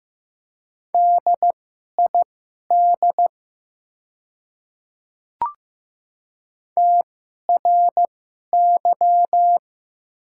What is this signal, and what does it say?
Morse code, heard as a keyed tone, about 15 wpm, spaced as for 11 wpm, 700 Hz. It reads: DID TRY
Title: 2x Character Spacing - Top 200 Words - 15wpm